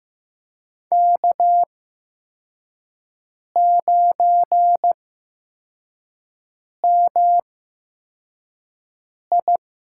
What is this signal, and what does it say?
Morse code, heard as a keyed tone, about 15 wpm, spaced as for 4 wpm, 700 Hz. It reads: K9MI
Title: USA Call Signs - 5 Characters - 8x Character Spacing - 15wpm